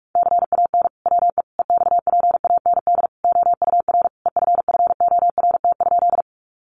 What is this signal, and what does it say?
Morse code, heard as a keyed tone, about 44 wpm, 700 Hz. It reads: CAN WE EXPAND OUR EFFORT?
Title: Sentences from Top 1400 Words 45wpm